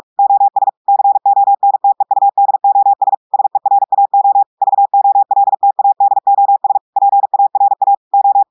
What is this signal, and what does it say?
Morse code, heard as a keyed tone, about 46 wpm, 800 Hz. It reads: OS CONTEUDOS SERAO VOLTADOS PARA O
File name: audios / portugais / portugais texte en clair 2 vitesse 45 mots minutes